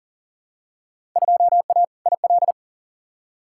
Morse code, heard as a keyed tone, about 40 wpm, 700 Hz. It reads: 2A IL
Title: Field Day Exchanges Encore - 40wpm